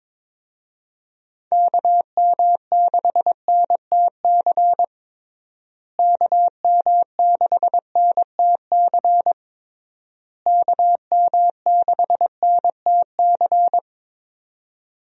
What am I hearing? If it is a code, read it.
Morse code, heard as a keyed tone, about 21 wpm, 700 Hz. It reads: KM6NTC KM6NTC KM6NTC
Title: USA Call Signs - 6 Characters - Repeated 3x - 22wpm